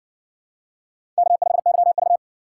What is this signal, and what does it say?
Morse code, heard as a keyed tone, about 60 wpm, 700 Hz. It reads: K5YV